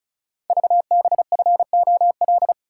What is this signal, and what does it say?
Morse code, heard as a keyed tone, about 36 wpm, 700 Hz. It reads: VBFOL
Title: Code Groups - 5 Letters - 35wpm